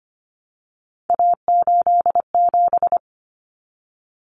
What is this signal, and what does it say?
Morse code, heard as a keyed tone, about 25 wpm, 700 Hz. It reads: A87
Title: Common Cut Numbers - 3-Digit 25wpm